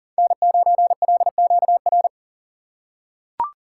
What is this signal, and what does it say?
Morse code, heard as a keyed tone, about 40 wpm, 700 Hz. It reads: N9LQR
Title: USA Call Signs - 5 Characters - 3x Character Spacing - 40wpm